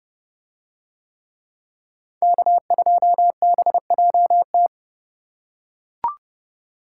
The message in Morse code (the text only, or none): K2BJT